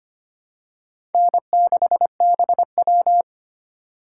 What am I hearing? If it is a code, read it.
Morse code, heard as a keyed tone, about 25 wpm, 700 Hz. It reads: N6BW
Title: USA Call Signs - 4 Characters - 2x Character Spacing - 25wpm